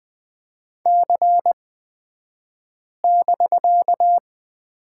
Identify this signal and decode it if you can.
Morse code, heard as a keyed tone, about 20 wpm, spaced as for 5 wpm, 700 Hz. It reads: C<BK>
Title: 249 Lesson 40d - BK - Two Characters 20wpm